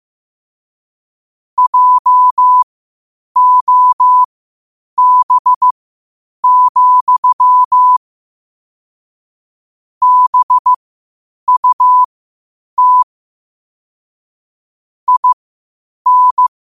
Morse code, heard as a keyed tone, about 15 wpm, spaced as for 8 wpm, 1000 Hz. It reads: JOB, BUT IN
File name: qotdfiles / 8wpm